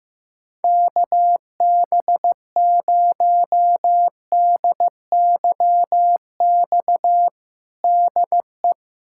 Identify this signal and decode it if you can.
Morse code, heard as a keyed tone, about 16 wpm, 700 Hz. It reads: KB0DYX DE
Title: USA Callsign DE Callsign - 6 Characters - 15wpm